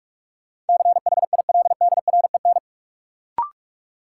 Morse code, heard as a keyed tone, about 47 wpm, 700 Hz. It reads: CHILDREN
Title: Sentences from Top 300 Words - New Word Drill - 45wpm